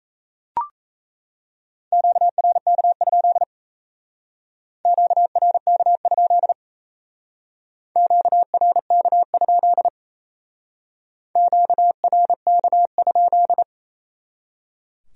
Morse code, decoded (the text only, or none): QRK? QRK? QRK? QRK?